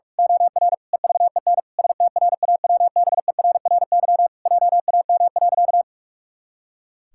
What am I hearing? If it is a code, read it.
Morse code, heard as a keyed tone, about 46 wpm, 700 Hz. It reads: OR EVEN STRAWBERRY JAM.
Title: Pangram Challenge v2 - 45wpm